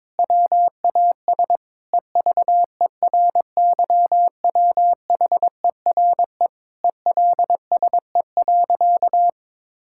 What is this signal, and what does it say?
Morse code, heard as a keyed tone, about 22 wpm, 700 Hz. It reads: WAS EVERYWHERE ELSE.
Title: Sentences from Top 1600 Words 22wpm